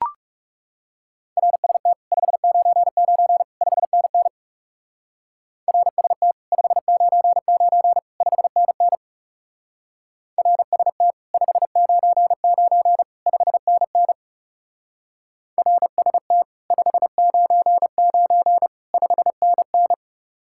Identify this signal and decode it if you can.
Morse code, heard as a keyed tone, about 39 wpm, 700 Hz. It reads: RST 599 5NN RST 599 5NN RST 599 5NN RST 599 5NN